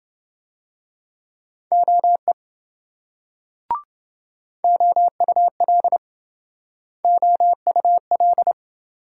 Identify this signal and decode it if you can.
Morse code, heard as a keyed tone, about 27 wpm, 700 Hz. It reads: OE OUL OUL